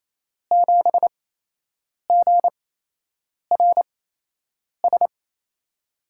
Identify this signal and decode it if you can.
Morse code, heard as a keyed tone, about 27 wpm, spaced as for 8 wpm, 700 Hz. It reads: 7GRS
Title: USA Call Signs - 6 Characters - 8x Character Spacing - 28wpm